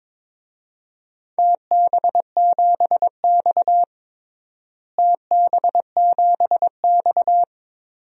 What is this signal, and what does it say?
Morse code, heard as a keyed tone, about 22 wpm, 700 Hz. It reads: TB7X TB7X
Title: DXCC-100 Call Signs - 4 Characters - Repeated 2x - 22wpm